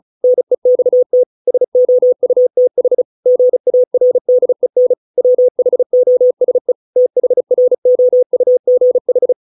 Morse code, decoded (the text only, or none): NEXT SOUTH GARDEN WHOSE THROUGH